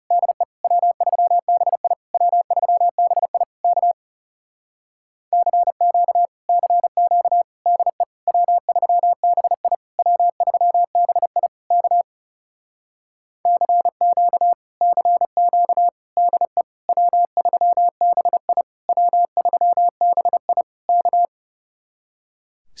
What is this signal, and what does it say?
Morse code, heard as a keyed tone, about 34 wpm, 700 Hz. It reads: DE W3BI W3BI K CQ CQ DE W3BI W3BI K CQ CQ DE W3BI W3BI K
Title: Speed-Racing Common QSO Phrases - 30wpm